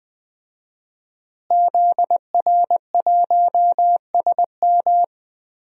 Morse code, heard as a keyed tone, about 20 wpm, 700 Hz. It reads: ZR1SM
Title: DXCC-100 Call Signs - 5 Characters - Repeated 3x - 20wpm